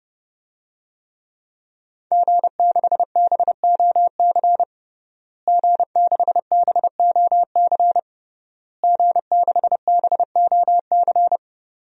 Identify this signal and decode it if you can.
Morse code, heard as a keyed tone, about 30 wpm, 700 Hz. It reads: G6BOC G6BOC G6BOC
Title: DXCC-100 Call Signs - 5 Characters - Repeated 3x - 30wpm